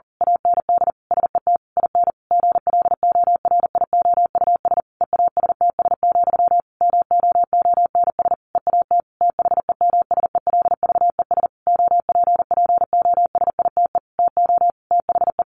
Morse code, heard as a keyed tone, about 40 wpm, 700 Hz. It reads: AND SET IN GLORIOUS EASTS, MOONS EAT THEMSELVES OPPOSITE TO THE
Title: A Sound Of Thunder 40wpm